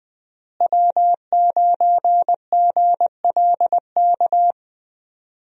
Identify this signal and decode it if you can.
Morse code, heard as a keyed tone, about 20 wpm, 700 Hz. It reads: W9GLK
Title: USA Call Signs - 5 Characters - Repeated 2x - 20wpm